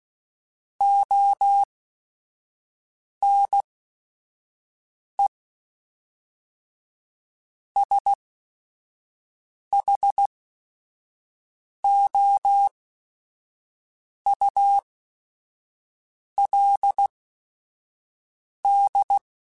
Morse code, heard as a keyed tone, about 16 wpm, spaced as for 5 wpm, 800 Hz. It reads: ONE SHOULD